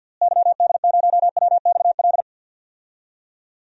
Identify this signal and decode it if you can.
Morse code, heard as a keyed tone, about 53 wpm, 700 Hz. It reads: YD0WXL
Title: DXCC-100 Call Signs - 6 Characters - Repeated 2x - 50wpm